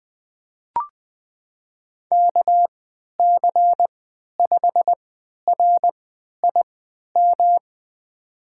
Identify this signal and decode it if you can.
Morse code, heard as a keyed tone, about 20 wpm, spaced as for 11 wpm, 700 Hz. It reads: KC5RIM
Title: USA Call Signs - 6 Characters - 3x Character Spacing - 20wpm